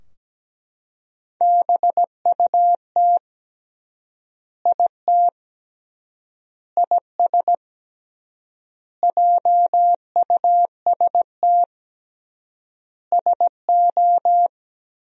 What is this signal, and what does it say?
Morse code, heard as a keyed tone, about 17 wpm, 700 Hz. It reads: BUT IT IS JUST SO
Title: 3x Word-Spacing - Sentences from Top 200 Words - 17wpm